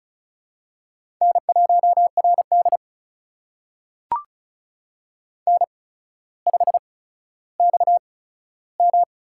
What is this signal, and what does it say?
Morse code, heard as a keyed tone, about 36 wpm, spaced as for 15 wpm, 700 Hz. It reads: N1RD N5XM
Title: USA Call Signs - 8x Character Spacing - 35wpm